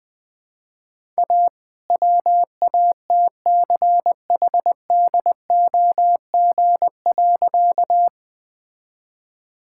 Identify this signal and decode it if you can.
Morse code, heard as a keyed tone, about 20 wpm, 700 Hz. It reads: A WATCHDOG.
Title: All-Age Appropriate Jokes - 20wpm